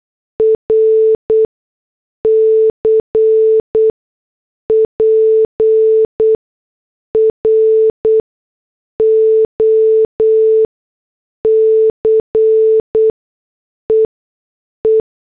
Morse code, heard as a keyed tone, about 8 wpm, spaced as for 6 wpm, 440 Hz. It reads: RC PROCEE
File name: morse / headlines